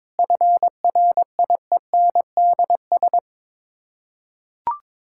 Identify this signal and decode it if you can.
Morse code, heard as a keyed tone, about 22 wpm, 700 Hz. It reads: FRIENDS
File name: practice / 22wpm / Sets of 7 Words - 22wpm